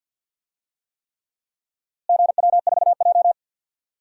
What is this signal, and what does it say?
Morse code, heard as a keyed tone, about 52 wpm, 700 Hz. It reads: GW4J
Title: DXCC-100 Call Signs - 4 Characters - Repeated 3x - 50wpm